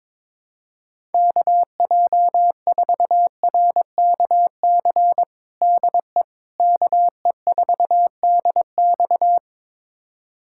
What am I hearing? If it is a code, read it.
Morse code, heard as a keyed tone, about 22 wpm, 700 Hz. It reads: KJ4RKC DE KE4DX
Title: USA Callsign DE Callsign 22wpm